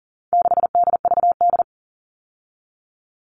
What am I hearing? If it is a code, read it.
Morse code, heard as a keyed tone, about 40 wpm, 700 Hz. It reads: 6DVD
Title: All Cut Numbers - 4-Digit 40wpm